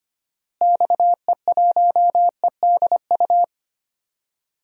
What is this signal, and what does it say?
Morse code, heard as a keyed tone, about 25 wpm, 700 Hz. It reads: XE1EDU
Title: DXCC-100 Call Signs - 6 Characters - Repeated 3x - 25wpm